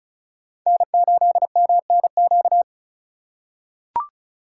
Morse code, heard as a keyed tone, about 34 wpm, 700 Hz. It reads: N8MNQ
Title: USA Call Signs - 5 Characters - 2x Character Spacing - 35wpm